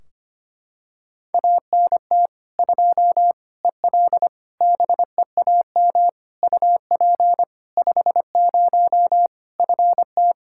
Morse code, heard as a keyed tone, about 25 wpm, 700 Hz. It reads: ANT 2 EL BEAM UP 50 FT
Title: Practice QSO - Exchange 2 - Info - 25wpm